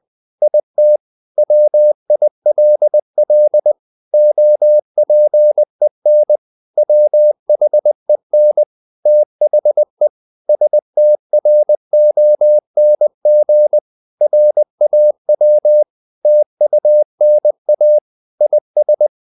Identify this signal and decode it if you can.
Morse code, heard as a keyed tone, about 20 wpm, 600 Hz. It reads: IT WILL OPEN WHEN THE STRONG RAW TUNA IS
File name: the old man and the sea / the old man and the sea part 1 20wpm